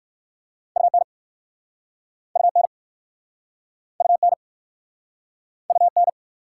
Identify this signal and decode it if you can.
Morse code, heard as a keyed tone, about 51 wpm, 690 Hz. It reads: UN UN UN UN
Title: Speed-Racing Prefixes and Suffixes 45wpm